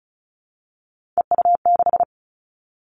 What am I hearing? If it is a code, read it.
Morse code, heard as a keyed tone, about 36 wpm, 690 Hz. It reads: EU6
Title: Callsign Prefixes for Top 50 DXCC - 35wpm